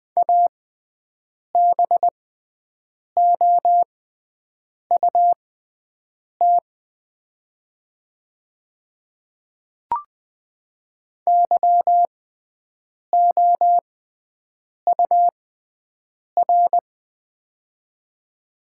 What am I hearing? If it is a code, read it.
Morse code, heard as a keyed tone, about 20 wpm, spaced as for 7 wpm, 700 Hz. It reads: ABOUT YOUR
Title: 6x Character Spacing - Top 100 Words - 20wpm